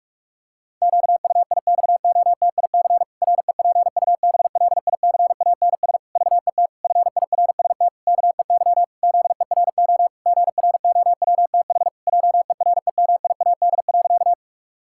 Morse code, heard as a keyed tone, about 46 wpm, 700 Hz. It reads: QUIXOTIC REPUBLICANS VET FIRST KEY ZERO GROWTH JEREMIAD.
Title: Pangram Challenge v2 - 45wpm